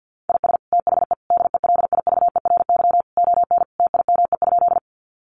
Morse code, heard as a keyed tone, about 48 wpm, 700 Hz. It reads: IS THE DELIVERY ON TIME?
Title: Sentences from Top 1500 Words 50wpm